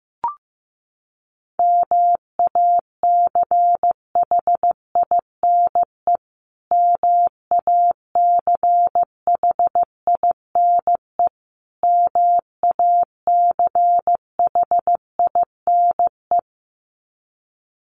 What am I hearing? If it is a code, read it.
Morse code, heard as a keyed tone, about 15 wpm, 700 Hz. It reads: MACHINE MACHINE MACHINE
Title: Top 500 Words Repeated 15wpm